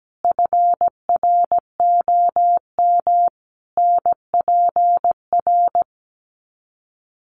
Morse code, heard as a keyed tone, about 17 wpm, 700 Hz. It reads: FROM NPR
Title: Sets of 5 Words Encore 17wpm